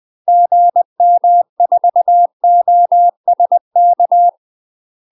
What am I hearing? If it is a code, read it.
Morse code, heard as a keyed tone, about 20 wpm, 700 Hz. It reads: GM4OSK
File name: UK MASTER QSOs TRAINING / 20WPM / UK MASTER QSOs 0028